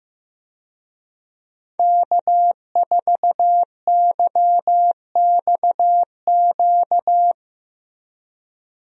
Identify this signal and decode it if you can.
Morse code, heard as a keyed tone, about 15 wpm, 700 Hz. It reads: K4YXQ
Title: USA Call Signs - 5 Characters - Repeated 3x - 15wpm